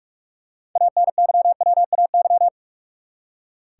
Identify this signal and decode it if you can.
Morse code, heard as a keyed tone, about 46 wpm, 700 Hz. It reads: ANYWAY
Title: Six Letter Words - 45wpm